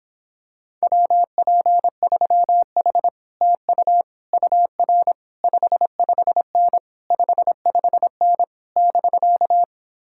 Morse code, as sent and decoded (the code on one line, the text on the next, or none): .-- .--. ...-- .... / - ..- / ..- .-. / ..... ..... -. / ..... ..... -. / -...-.-
WP3H TU UR 55N 55N <BK>